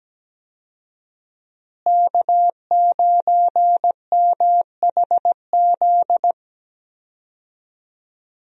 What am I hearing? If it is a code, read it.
Morse code, heard as a keyed tone, about 17 wpm, 700 Hz. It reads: K9MHZ
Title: USA Call Signs - 5 Characters - Repeated 2x - 17wpm